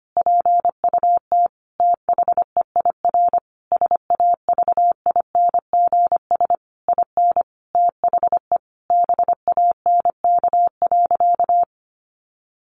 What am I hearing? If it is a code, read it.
Morse code, heard as a keyed tone, about 25 wpm, 700 Hz. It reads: PUT THEIR SAVINGS IN THE BANK.
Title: Sentences from Top 1200 Words 25wpm